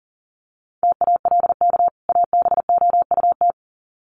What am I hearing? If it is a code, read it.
Morse code, heard as a keyed tone, about 40 wpm, 700 Hz. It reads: TALK ABOUT
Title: Sets of 2 Words 40wpm